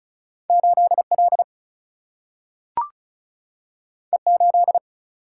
Morse code, heard as a keyed tone, about 36 wpm, 700 Hz. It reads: 8L E8
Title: Single-Letter Single-Number 35wpm